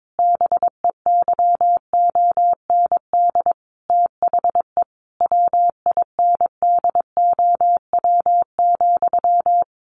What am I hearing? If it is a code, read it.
Morse code, heard as a keyed tone, about 22 wpm, 700 Hz. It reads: BEYOND THE WINDOW,
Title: A Sound Of Thunder 22wpm